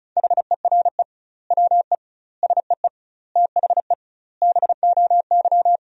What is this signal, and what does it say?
Morse code, heard as a keyed tone, about 36 wpm, 700 Hz. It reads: HERE WE SEE THE BOY
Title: 2x Word-Spacing - Sentences from Top 300 Words - 35wpm